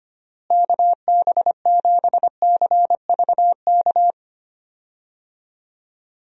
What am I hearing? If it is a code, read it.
Morse code, heard as a keyed tone, about 25 wpm, 700 Hz. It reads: KB7CVK
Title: USA Call Signs - 6 Characters - Repeated 2x - 25wpm